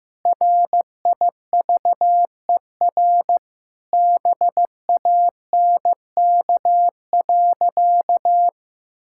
Morse code, heard as a keyed tone, about 15 wpm, 700 Hz. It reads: RIVER BANK.
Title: Sentences from Top 1000 Words - Review Part 6 - 15wpm